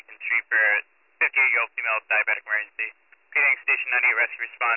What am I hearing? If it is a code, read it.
Emergency Alert System (SAME) tones (header bursts), heard as harsh buzzing bursts, with speech. It is heard from 0.2 to 4.5 s.